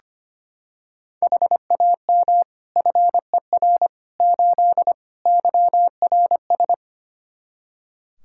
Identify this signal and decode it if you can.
Morse code, heard as a keyed tone, about 25 wpm, 700 Hz. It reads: HAM FER 8 YRS
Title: Practice QSO - Exchange 3 - More Info - 25wpm